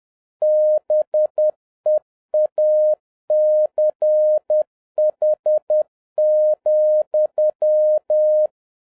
Morse code, heard as a keyed tone, about 10 wpm, 610 Hz. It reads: BEACH,
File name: 2011-07-mp3 / n31 THE 3.11 2 10wpm